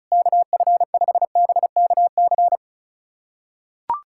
Morse code, heard as a keyed tone, about 36 wpm, 700 Hz. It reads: KF5BKC
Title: USA Callsign DE Callsign - 6 Characters - 35wpm